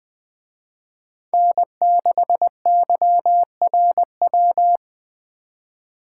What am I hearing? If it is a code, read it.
Morse code, heard as a keyed tone, about 20 wpm, 700 Hz. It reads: N6YRW
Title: Speed Racing - USA 1x3 Call Signs 17wpm